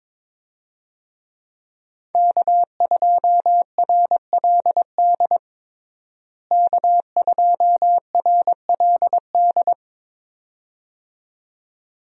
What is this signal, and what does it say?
Morse code, heard as a keyed tone, about 22 wpm, 700 Hz. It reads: K2RLD K2RLD
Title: USA Call Signs - 5 Characters - Repeated 2x - 22wpm